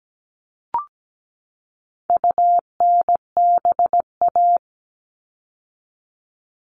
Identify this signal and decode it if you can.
Morse code, heard as a keyed tone, about 17 wpm, 700 Hz. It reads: UNBA
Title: All Cut Numbers - 4-Digit 17wpm